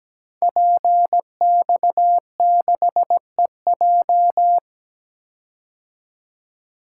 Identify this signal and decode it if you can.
Morse code, heard as a keyed tone, about 17 wpm, 700 Hz. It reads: PX6EJ
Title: Code Groups - 5 Letter:Numbers - 17wpm